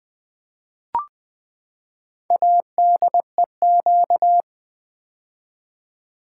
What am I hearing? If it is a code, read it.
Morse code, heard as a keyed tone, about 20 wpm, 700 Hz. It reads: ADEQ
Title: Build A Word - 4 Letters At A Time - 10-Letter Words - 20wpm